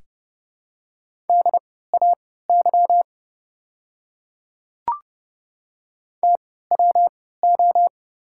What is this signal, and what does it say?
Morse code, heard as a keyed tone, about 30 wpm, spaced as for 17 wpm, 700 Hz. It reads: DAY TWO
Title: 3x Character Spacing - Top 200 Words - 30wpm